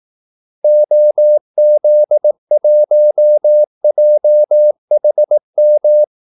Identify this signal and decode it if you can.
Morse code, heard as a keyed tone, about 18 wpm, 600 Hz. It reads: OZ1JHM
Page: Kaldesignaler 18 wpm